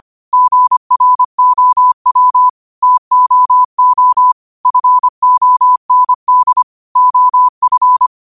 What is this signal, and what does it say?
Morse code, heard as a keyed tone, about 25 wpm, 1000 Hz. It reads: GROW TOO FOND OF
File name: qotdfiles / 25wpm